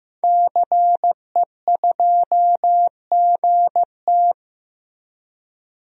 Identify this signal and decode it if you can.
Morse code, heard as a keyed tone, about 16 wpm, 700 Hz. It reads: CE2GT
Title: DXCC-100 Call Signs - 5 Characters - 15wpm